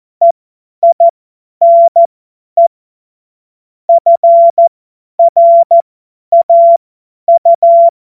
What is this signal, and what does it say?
Morse code, heard as a keyed tone, about 14 wpm, spaced as for 10 wpm, 680 Hz. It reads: EINE FRAU
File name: Morsekurs / Witze / Witz 13 Februar 2024 50 F70